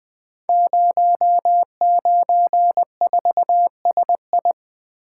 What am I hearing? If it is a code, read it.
Morse code, heard as a keyed tone, about 20 wpm, 700 Hz. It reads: 094SI
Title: Code Groups - 5 Letter:Numbers - 20wpm